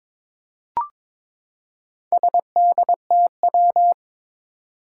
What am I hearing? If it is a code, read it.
Morse code, heard as a keyed tone, about 22 wpm, 700 Hz. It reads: SDTW
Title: Four Challenging Letter-Number Combinations 22wpm